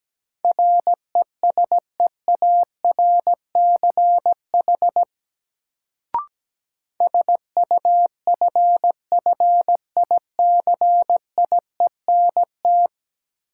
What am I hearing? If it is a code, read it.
Morse code, heard as a keyed tone, about 17 wpm, 700 Hz. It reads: RESEARCH SUFFICIENT
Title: Sentences from Top 1400 Words - New Word Drill Alt - 17wpm